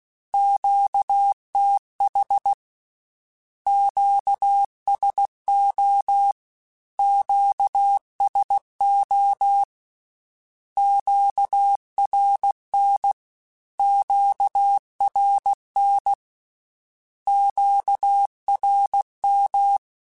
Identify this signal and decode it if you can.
Morse code, heard as a keyed tone, about 16 wpm, 800 Hz. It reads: QTH QSO QSO QRN QRN QRM
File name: Q Calls / Q Calls-mp3s / Q Calls Twice @15